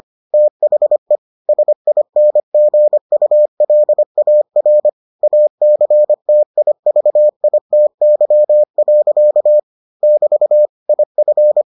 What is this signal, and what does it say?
Morse code, heard as a keyed tone, about 25 wpm, 600 Hz. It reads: THE SINGULAR ACTIVITY. <BT> IF